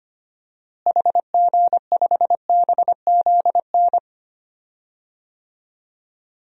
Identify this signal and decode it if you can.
Morse code, heard as a keyed tone, about 25 wpm, 700 Hz. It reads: HG5BZN
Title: DXCC-100 Call Signs - 6 Characters - Repeated 3x - 25wpm